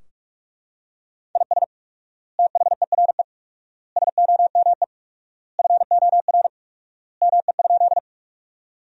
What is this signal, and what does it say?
Morse code, heard as a keyed tone, about 43 wpm, 700 Hz. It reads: IS THERE SOME FOR ME?
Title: 4x Word-Spacing - Sentences from Top 200 Words - 45wpm